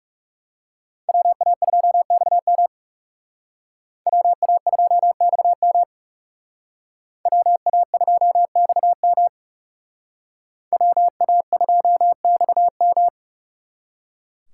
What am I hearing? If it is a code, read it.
Morse code, heard as a keyed tone, about 34 wpm, 700 Hz. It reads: WA2XM WA2XM WA2XM WA2XM
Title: Speed Racing - USA 2x2 Call Signs 30wpm